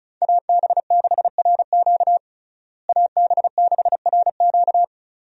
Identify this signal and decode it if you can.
Morse code, heard as a keyed tone, about 36 wpm, 700 Hz. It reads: AB6RQ AB6RQ
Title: USA Call Signs - 5 Characters - Repeated 3x - 35wpm